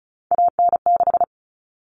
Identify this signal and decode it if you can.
Morse code, heard as a keyed tone, about 34 wpm, 700 Hz. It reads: AN6
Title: Common Cut Numbers - 3-Digit 35wpm